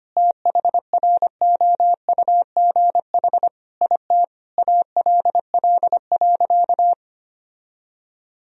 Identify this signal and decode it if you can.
Morse code, heard as a keyed tone, about 25 wpm, 700 Hz. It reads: THROUGH IT ALL.
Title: Sentences from Top 1000 Words - Review Part 1 - 25wpm